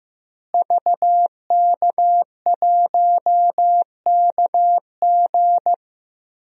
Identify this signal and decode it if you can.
Morse code, heard as a keyed tone, about 15 wpm, 700 Hz. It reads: VK1KG
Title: DXCC-100 Call Signs - 5 Characters - Repeated 2x - 15wpm